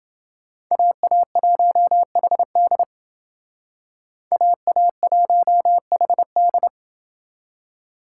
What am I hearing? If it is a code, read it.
Morse code, heard as a keyed tone, about 29 wpm, 700 Hz. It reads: AA1HD AA1HD